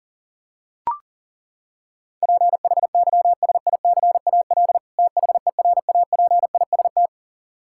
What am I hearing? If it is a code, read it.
Morse code, heard as a keyed tone, about 40 wpm, 700 Hz. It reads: PHYSICAL THERAPIST
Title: Practice QSO - Exchange 3 - More Info - 40wpm